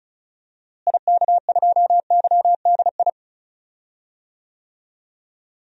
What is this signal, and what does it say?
Morse code, heard as a keyed tone, about 36 wpm, 700 Hz. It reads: IK2YDI
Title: DXCC-100 Call Signs - 6 Characters - Repeated 3x - 35wpm